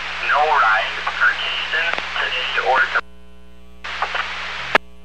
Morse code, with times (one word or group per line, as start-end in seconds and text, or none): none